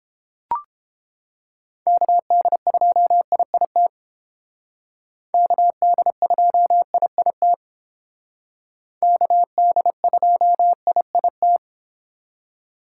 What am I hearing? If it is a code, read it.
Morse code, heard as a keyed tone, about 30 wpm, 700 Hz. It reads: KD2IIT KD2IIT KD2IIT